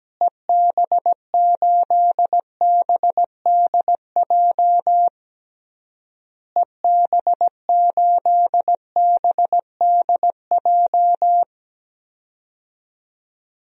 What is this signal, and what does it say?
Morse code, heard as a keyed tone, about 17 wpm, 700 Hz. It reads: EB8BDJ EB8BDJ
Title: DXCC-100 Call Signs - 6 Characters - Repeated 3x - 17wpm